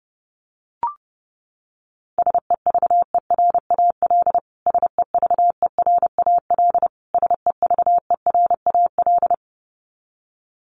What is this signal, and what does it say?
Morse code, heard as a keyed tone, about 30 wpm, 700 Hz. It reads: SEVERAL SEVERAL SEVERAL
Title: Top 500 Words Repeated 30wpm